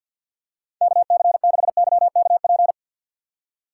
Morse code, heard as a keyed tone, about 48 wpm, 700 Hz. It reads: KC6YKP
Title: USA Call Signs - 6 Characters - 3x Character Spacing - 50wpm